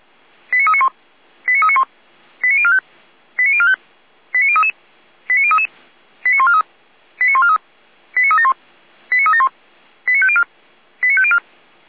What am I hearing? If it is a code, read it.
Five-tone selective call (ZVEI), heard as quick runs of five tones, tones of about 70 ms. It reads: 89391 89391 89045 89045 89033 89033 89143 89143 89371 89371 89504 89504